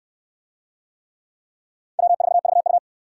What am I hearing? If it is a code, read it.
Morse code, heard as a keyed tone, about 68 wpm, 700 Hz. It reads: W4VU